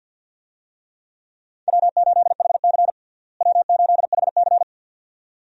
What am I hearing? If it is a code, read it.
Morse code, heard as a keyed tone, about 52 wpm, 700 Hz. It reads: W8HC W8HC
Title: USA Call Signs - 4 Characters - Repeated 2x - 50wpm